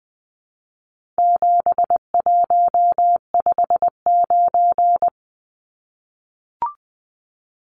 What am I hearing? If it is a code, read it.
Morse code, heard as a keyed tone, about 19 wpm, 700 Hz. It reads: 7159
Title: Numbers - 4-Digit 20wpm